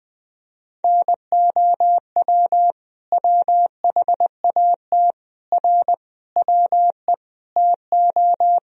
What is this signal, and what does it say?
Morse code, heard as a keyed tone, about 21 wpm, 700 Hz. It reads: NOW WHAT R WE TO
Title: Sentences from Top 100 Words and Abbreviations 20wpm